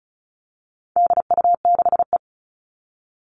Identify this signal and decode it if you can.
Morse code, heard as a keyed tone, about 34 wpm, 700 Hz. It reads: DU6E
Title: All Cut Numbers - 4-Digit 35wpm